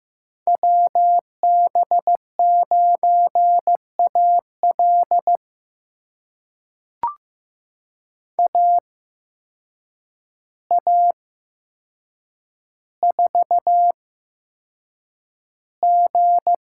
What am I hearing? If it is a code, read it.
Morse code, heard as a keyed tone, about 15 wpm, spaced as for 7 wpm, 700 Hz. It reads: WB9AL AA4G
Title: USA Call Signs - 5 Characters - 8x Character Spacing - 15wpm